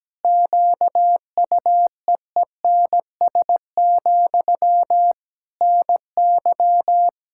Morse code, heard as a keyed tone, about 17 wpm, 700 Hz. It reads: QUEENS, NY
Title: K1USN SST QSO Practice 17wpm